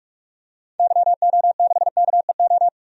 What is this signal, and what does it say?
Morse code, heard as a keyed tone, about 46 wpm, 700 Hz. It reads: YO6KEO
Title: DXCC-100 Call Signs - 6 Characters - 45wpm